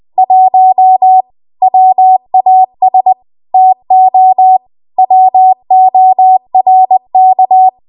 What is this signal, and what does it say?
Morse code, heard as a keyed tone, about 20 wpm, 750 Hz. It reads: 1 WAS TO WORK
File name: Morse / 20WPM